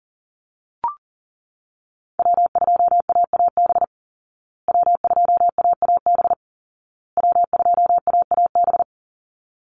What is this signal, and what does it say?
Morse code, heard as a keyed tone, about 40 wpm, 700 Hz. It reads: W2AAB W2AAB W2AAB
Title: Call Signs Repeated 2x - 40wpm